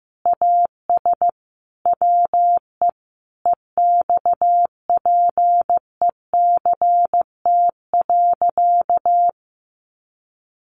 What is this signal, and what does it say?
Morse code, heard as a keyed tone, about 15 wpm, 700 Hz. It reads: AS WE EXPECT.
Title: Sentences from Top 1100 Words 15wpm